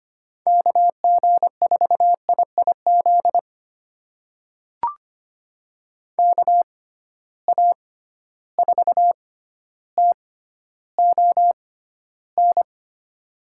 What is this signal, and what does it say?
Morse code, heard as a keyed tone, about 25 wpm, spaced as for 13 wpm, 700 Hz. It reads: KG4IIZ KA4TON
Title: USA Call Signs - 6 Characters - 6x Character Spacing - 25wpm